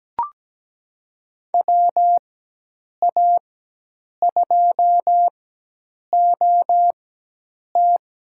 Morse code, heard as a keyed tone, about 17 wpm, spaced as for 8 wpm, 700 Hz. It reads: WA2OT
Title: USA Call Signs - 4x Character Spacing - 17wpm